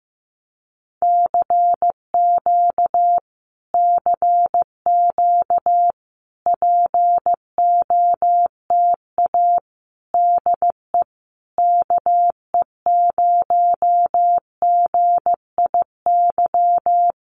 Morse code, heard as a keyed tone, about 15 wpm, 700 Hz. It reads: CQ CQ POTA DE KE0GIY
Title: Common QSO Phrases Encore 15wpm